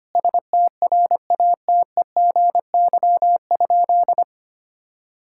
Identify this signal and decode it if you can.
Morse code, heard as a keyed tone, about 25 wpm, 700 Hz. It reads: STRATEGY?
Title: Sentences from Top 1000 Words - Review Part 5 - 25wpm